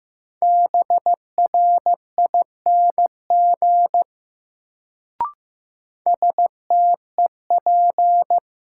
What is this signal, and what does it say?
Morse code, heard as a keyed tone, about 15 wpm, 700 Hz. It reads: BRING STEP
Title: Sets of 5 Words Encore - New Word Drill - 15wpm